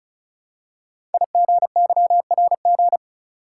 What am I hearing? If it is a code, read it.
Morse code, heard as a keyed tone, about 36 wpm, 700 Hz. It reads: IGYRG
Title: Code Groups - 5 Letters - 35wpm